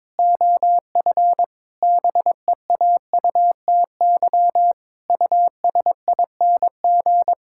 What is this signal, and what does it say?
Morse code, heard as a keyed tone, about 22 wpm, 700 Hz. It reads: OF BEAUTY USING
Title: Pangram Challenge v2 - 22wpm